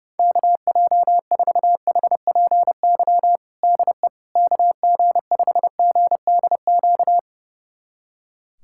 Morse code, heard as a keyed tone, about 30 wpm, 700 Hz. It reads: KJ4HPY DE KG5GDQ
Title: USA Callsign DE Callsign 30wpm